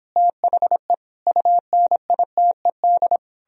Morse code, heard as a keyed tone, about 29 wpm, 700 Hz. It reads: THE UNITED